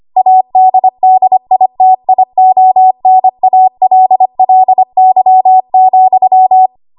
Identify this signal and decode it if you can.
Morse code, heard as a keyed tone, about 25 wpm, 750 Hz. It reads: ADDITIONALLY,
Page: 25 WPM Code Archive